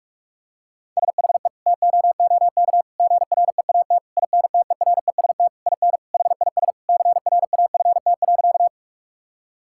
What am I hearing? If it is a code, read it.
Morse code, heard as a keyed tone, about 41 wpm, 700 Hz. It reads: SHE TOOK GREAT INTEREST IN HIS CRAFT.
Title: Sentences from Top 1000 Words - Review Part 6 - 45wpm